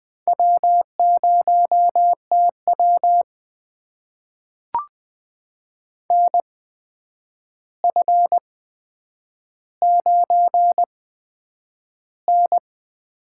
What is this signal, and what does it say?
Morse code, heard as a keyed tone, about 20 wpm, spaced as for 9 wpm, 700 Hz. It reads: W0TW NF9N
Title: USA Call Signs - 4 Characters - 8x Character Spacing - 20wpm